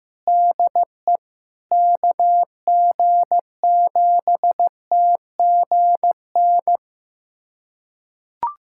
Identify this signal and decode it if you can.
Morse code, heard as a keyed tone, about 15 wpm, 700 Hz. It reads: DE KG7TGN
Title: USA Callsign DE Callsign 15wpm